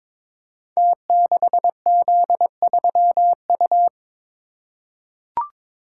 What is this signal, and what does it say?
Morse code, heard as a keyed tone, about 22 wpm, 690 Hz. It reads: T6Z3U
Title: Code Groups - 5 Letter:Numbers - 22wpm